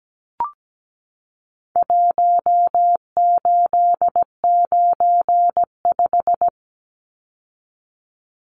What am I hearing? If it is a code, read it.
Morse code, heard as a keyed tone, about 17 wpm, 700 Hz. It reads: 1895
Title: Numbers - Contest Style - 4-Digit 17wpm